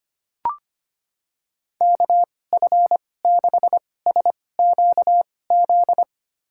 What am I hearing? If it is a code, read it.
Morse code, heard as a keyed tone, about 25 wpm, spaced as for 18 wpm, 700 Hz. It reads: KF6SQZ
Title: USA Call Signs - 6 Characters - 2x Character Spacing - 25wpm